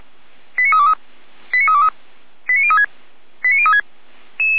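Five-tone selective call (ZVEI), heard as quick runs of five tones, tones of about 70 ms. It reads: 89323 89323 89036 89036
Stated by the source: ZVEI five-tone call received on an SDR.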